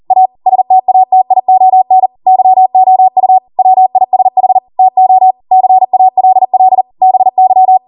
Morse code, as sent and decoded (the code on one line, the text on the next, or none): .- / ... - .- - .. --- -. / -.-- --- ..- / .-- .. ... .... / - --- / -.-. .- .-.. .-.. / -... -.--
A STATION YOU WISH TO CALL BY